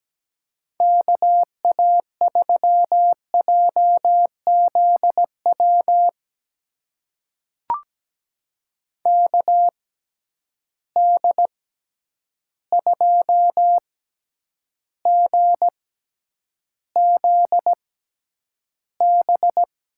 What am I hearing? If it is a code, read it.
Morse code, heard as a keyed tone, about 17 wpm, spaced as for 9 wpm, 700 Hz. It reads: KA3JZW KD2GZB
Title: USA Call Signs - 6 Characters - 6x Character Spacing - 17wpm